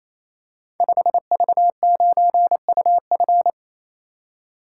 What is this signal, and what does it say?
Morse code, heard as a keyed tone, about 27 wpm, 700 Hz. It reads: 5V9UF
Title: Code Groups - 5 Letter:Numbers - 28wpm